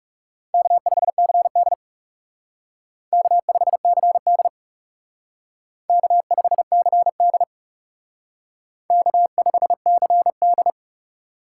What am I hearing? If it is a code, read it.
Morse code, heard as a keyed tone, about 36 wpm, 700 Hz. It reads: K5CD K5CD K5CD K5CD
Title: Speed Racing - USA 1x2 Call Signs 30wpm